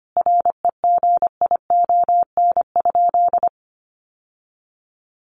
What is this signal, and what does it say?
Morse code, heard as a keyed tone, about 25 wpm, 700 Hz. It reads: REGION?
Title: Sentences from Top 1500 Words 25wpm